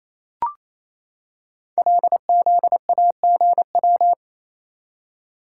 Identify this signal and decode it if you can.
Morse code, heard as a keyed tone, about 28 wpm, 700 Hz. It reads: LZAGW
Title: Code Groups - 5 Letter:Numbers - 28wpm